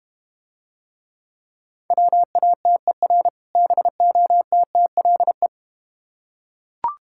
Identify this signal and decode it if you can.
Morse code, heard as a keyed tone, about 30 wpm, 700 Hz. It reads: WATER BOTTLE
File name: practice / 25wpm / Difficult Words - Speed-Racing with Hints - 25wpm